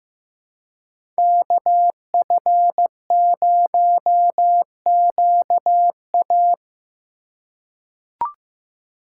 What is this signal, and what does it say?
Morse code, heard as a keyed tone, about 15 wpm, 700 Hz. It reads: KF0QA
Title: USA Call Signs - 5 Characters - 2x Character Spacing - 15wpm